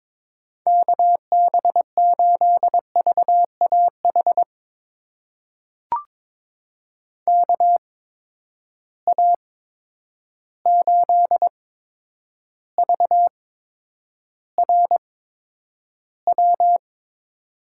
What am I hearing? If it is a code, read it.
Morse code, heard as a keyed tone, about 22 wpm, spaced as for 9 wpm, 700 Hz. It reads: KB8VAH KA8VRW
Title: USA Call Signs - 6 Characters - 8x Character Spacing - 22wpm